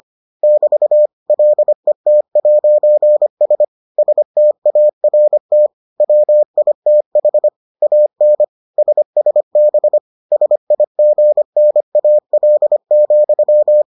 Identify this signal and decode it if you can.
Morse code, heard as a keyed tone, about 25 wpm, 600 Hz. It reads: <BT> LET'S START WITH AN SSB SIGNAL,